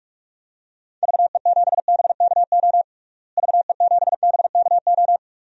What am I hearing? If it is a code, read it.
Morse code, heard as a keyed tone, about 46 wpm, 700 Hz. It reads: VE7BKO VE7BKO
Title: DXCC-100 Call Signs - 6 Characters - Repeated 3x - 45wpm